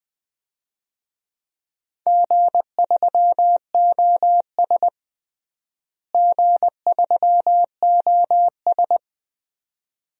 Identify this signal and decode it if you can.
Morse code, heard as a keyed tone, about 20 wpm, 700 Hz. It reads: G3OS G3OS
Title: DXCC-100 Call Signs - 4 Characters - Repeated 2x - 20wpm